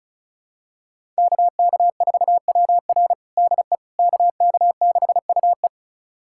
Morse code, heard as a keyed tone, about 36 wpm, 700 Hz. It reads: KK4WR DE KK6UE
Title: USA Callsign DE Callsign - 5 Characters - 35wpm